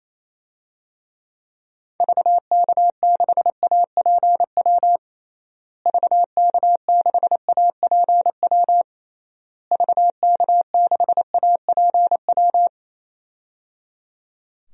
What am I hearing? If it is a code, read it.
Morse code, heard as a keyed tone, about 28 wpm, 700 Hz. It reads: VK6APW VK6APW VK6APW
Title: DXCC-100 Call Signs - 6 Characters - Repeated 3x - 28wpm